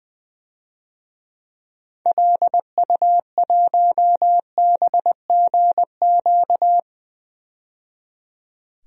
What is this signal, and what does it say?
Morse code, heard as a keyed tone, about 20 wpm, 700 Hz. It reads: LU1BGQ
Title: DXCC-100 Call Signs - 6 Characters - 20wpm